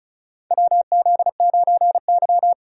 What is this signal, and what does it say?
Morse code, heard as a keyed tone, about 36 wpm, 700 Hz. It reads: WZ9Y
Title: USA Call Signs - 4 Characters - 2x Character Spacing - 35wpm